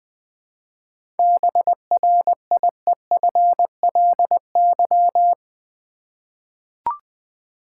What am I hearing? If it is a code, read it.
Morse code, heard as a keyed tone, about 20 wpm, 700 Hz. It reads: BRIEFLY
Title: Seven Letter Words - 20wpm